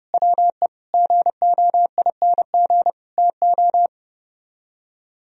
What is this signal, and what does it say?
Morse code, heard as a keyed tone, about 29 wpm, 700 Hz. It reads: WE GOING TO